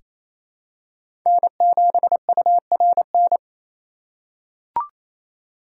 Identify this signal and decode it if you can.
Morse code, heard as a keyed tone, about 28 wpm, 700 Hz. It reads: N7URN
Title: USA Call Signs - 5 Characters - 6x Character Spacing - 28wpm